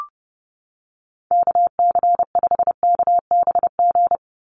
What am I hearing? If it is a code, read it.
Morse code, heard as a keyed tone, about 30 wpm, 700 Hz. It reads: KC5KBG
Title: USA Call Signs 30wpm